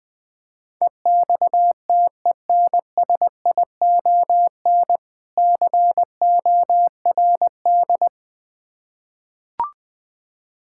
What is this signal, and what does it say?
Morse code, heard as a keyed tone, about 21 wpm, 690 Hz. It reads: EXTENSION CORD
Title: Difficult Words with Hints - 20wpm